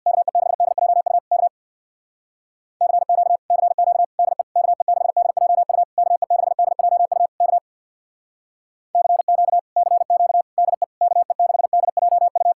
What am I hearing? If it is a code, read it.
Morse code, heard as a keyed tone, about 62 wpm, 700 Hz. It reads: KE6DJU K CQ CQ DE KE6DJU KE6DJU K CQ CQ DE KE6DJU
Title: Speed-Racing Common QSO Phrases Encore - 50wpm